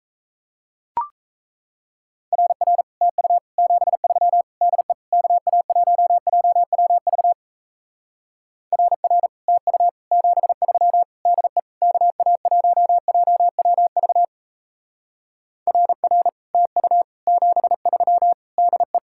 Morse code, heard as a keyed tone, about 37 wpm, 700 Hz. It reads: RR TU 73 DE KA1JWV RR TU 73 DE KA1JWV RR TU 73 DE